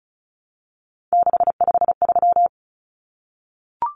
Morse code, heard as a keyed tone, about 34 wpm, 700 Hz. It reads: 653
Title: Common Cut Numbers - 3-Digit 35wpm